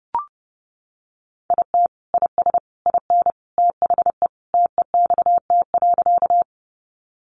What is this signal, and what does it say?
Morse code, heard as a keyed tone, about 30 wpm, 700 Hz. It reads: IT IS IN THE TEXT.
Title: Sentences from Top 1200 Words 30wpm